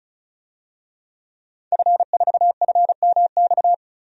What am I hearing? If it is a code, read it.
Morse code, heard as a keyed tone, about 34 wpm, 700 Hz. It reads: F4FMX
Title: DXCC-100 Call Signs - 5 Characters - 35wpm